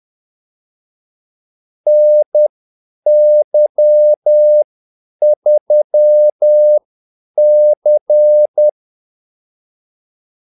Morse code, heard as a keyed tone, about 10 wpm, spaced as for 8 wpm, 600 Hz. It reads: NY3C
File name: Callsigns-10-8wpm